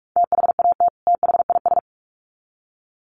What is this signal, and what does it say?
Morse code, heard as a keyed tone, about 44 wpm, 700 Hz. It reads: THAT THIS
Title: Sets of 2 Words 45wpm